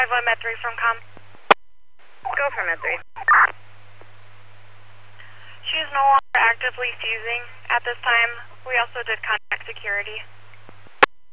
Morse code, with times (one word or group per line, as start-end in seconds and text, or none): none